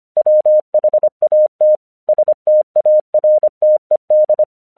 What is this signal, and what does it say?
Morse code, heard as a keyed tone, about 25 wpm, 610 Hz. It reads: WHAT STARTED